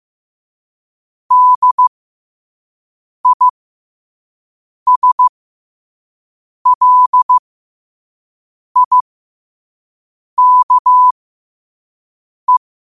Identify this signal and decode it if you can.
Morse code, heard as a keyed tone, about 15 wpm, spaced as for 5 wpm, 1000 Hz. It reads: DISLIKE
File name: qotdfiles / 5wpm